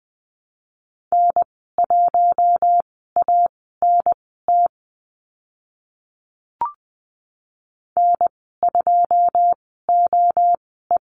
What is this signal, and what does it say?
Morse code, heard as a keyed tone, about 20 wpm, spaced as for 14 wpm, 700 Hz. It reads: N1ANT N2OE
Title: 082 Lesson 14g - Number 2 - Call Signs 20wpm